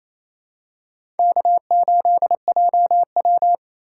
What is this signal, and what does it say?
Morse code, heard as a keyed tone, about 28 wpm, 700 Hz. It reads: K8JW
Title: USA Call Signs - 4 Characters - 4x Character Spacing - 28wpm